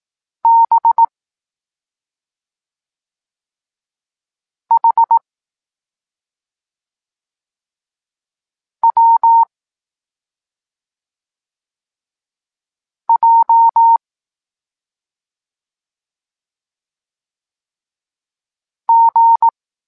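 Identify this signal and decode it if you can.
Morse code, heard as a keyed tone, about 18 wpm, 920 Hz. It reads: B H W J G
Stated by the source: Evenly spaced Morse lesson text 21 for learning letters G and Z, from SP9MOA Niepołomicki Klub Krótkofalowców.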